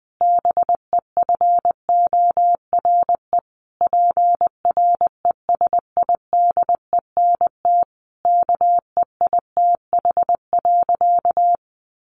Morse code, heard as a keyed tone, about 20 wpm, 700 Hz. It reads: BEFORE PRESIDENT KEITH.
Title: A Sound Of Thunder 20wpm